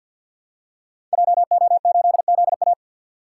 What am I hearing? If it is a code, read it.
Morse code, heard as a keyed tone, about 50 wpm, 700 Hz. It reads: JO8ZA
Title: Code Groups - 5 Letter:Numbers - 50wpm